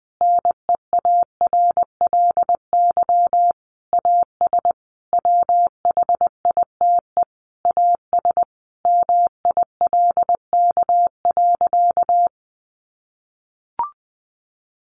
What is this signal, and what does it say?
Morse code, heard as a keyed tone, about 20 wpm, 700 Hz. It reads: NEARLY AS WHITE AS MILK.
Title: Sentences from Top 1500 Words 20wpm